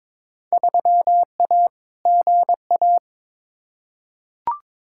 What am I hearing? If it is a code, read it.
Morse code, heard as a keyed tone, about 21 wpm, 700 Hz. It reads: 3A GA
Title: Field Day Exchanges - 22wpm